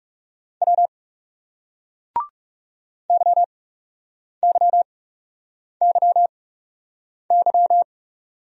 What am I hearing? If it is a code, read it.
Morse code, heard as a keyed tone, about 38 wpm, spaced as for 8 wpm, 700 Hz. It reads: W YYYY